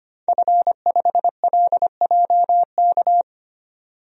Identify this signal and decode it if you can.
Morse code, heard as a keyed tone, about 25 wpm, 700 Hz. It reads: F5LJK
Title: DXCC-100 Call Signs - 5 Characters - 25wpm